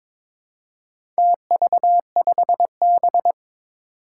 Morse code, heard as a keyed tone, about 22 wpm, 700 Hz. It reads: TV5B
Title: Four Challenging Letter-Number Combinations 22wpm